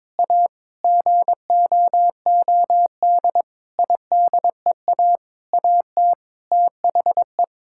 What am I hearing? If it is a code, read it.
Morse code, heard as a keyed tone, about 22 wpm, 700 Hz. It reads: A GOOD IDEA AT THE
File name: Sets of 9 Words - 22wpm